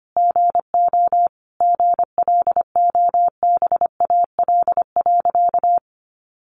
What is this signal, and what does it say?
Morse code, heard as a keyed tone, about 25 wpm, 700 Hz. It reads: GO GLOBAL.
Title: Sentences from Top 1200 Words 25wpm